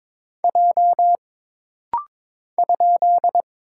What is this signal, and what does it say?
Morse code, heard as a keyed tone, about 22 wpm, 700 Hz. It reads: J ?
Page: This rapid-fire timing is geared to push your ICR (Instant Character Recognition) skills.